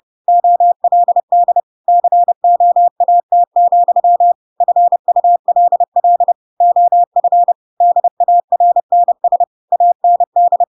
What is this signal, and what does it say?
Morse code, heard as a keyed tone, about 30 wpm, 700 Hz. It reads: OLD COAT, FULL OF DARNS AND
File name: Pinocchio / Pinocchio ch8 30 30 ik1hge